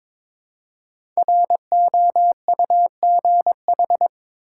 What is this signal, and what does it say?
Morse code, heard as a keyed tone, about 22 wpm, 700 Hz. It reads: ROUGH
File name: practice / 22wpm / 5 Letter Words - First 3 Letters Given - 22wpm